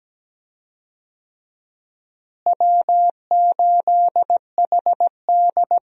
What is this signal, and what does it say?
Morse code, heard as a keyed tone, about 17 wpm, 700 Hz. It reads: W8HD
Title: USA Call Signs - 4 Characters - Repeated 3x - 17wpm